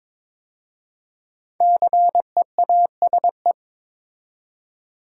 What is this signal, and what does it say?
Morse code, heard as a keyed tone, about 22 wpm, 700 Hz. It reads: CEASE
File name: 5 Letter Words - First 3 Letters Given - 22wpm